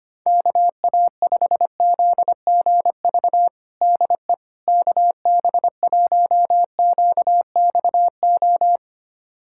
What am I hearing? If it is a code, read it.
Morse code, heard as a keyed tone, about 25 wpm, 700 Hz. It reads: KA5ZGV DE KB1QXO
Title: USA Callsign DE Callsign - 6 Characters - 25wpm